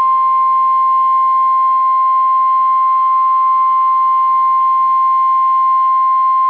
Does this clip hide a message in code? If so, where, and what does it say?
Yes: Emergency Alert System (SAME) tones (the 1050 Hz attention tone) from 0.4 to 5.8 s.